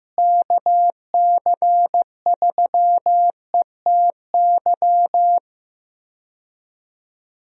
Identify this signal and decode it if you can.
Morse code, heard as a keyed tone, about 15 wpm, 700 Hz. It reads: KC3ETY
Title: USA Call Signs - 6 Characters - Repeated 2x - 15wpm